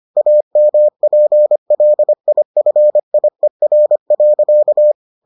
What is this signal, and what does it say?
Morse code, heard as a keyed tone, about 25 wpm, 600 Hz. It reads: AMPLIFIER.